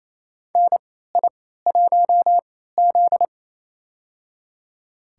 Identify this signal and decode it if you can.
Morse code, heard as a keyed tone, about 27 wpm, spaced as for 16 wpm, 700 Hz. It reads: NI1Z
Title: USA Call Signs - 4 Characters - 3x Character Spacing - 28wpm